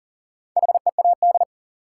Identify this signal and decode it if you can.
Morse code, heard as a keyed tone, about 40 wpm, 700 Hz. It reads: HEAD
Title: Sentences from Top 300 Words - New Word Drill - 40wpm